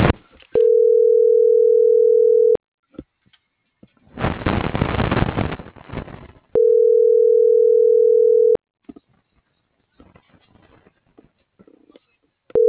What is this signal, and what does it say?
Ringback tone (North America), 440 + 480 Hz.